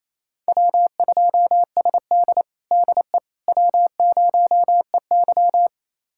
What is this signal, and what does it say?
Morse code, heard as a keyed tone, about 28 wpm, 690 Hz. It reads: W2SD DE W0EY
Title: USA Callsign DE Callsign - 4 Characters - 28wpm